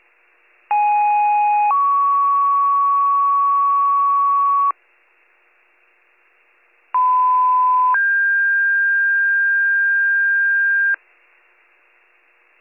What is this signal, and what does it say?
Two-tone page, heard as a one-second tone, then a three-second tone, about 830 then 1120 Hz, about 1010 then 1600 Hz. It is heard from 0.7 to 11.0 s.